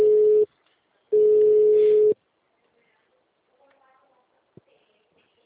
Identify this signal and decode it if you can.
Ringback tone (Europe, 425 Hz), with speech.
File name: out- -2017- - - .17